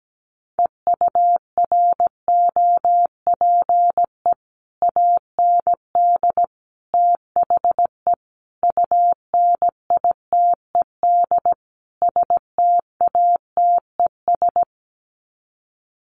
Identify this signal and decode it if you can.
Morse code, heard as a keyed tone, about 17 wpm, 700 Hz. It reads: EUROPE AND THE UNITED STATES
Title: Sets of 5 Words Encore2 17wpm